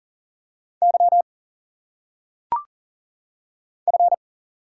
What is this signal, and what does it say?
Morse code, heard as a keyed tone, about 38 wpm, 700 Hz. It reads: Y F
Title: Single Letters - Phonetic Pronunciation 40wpm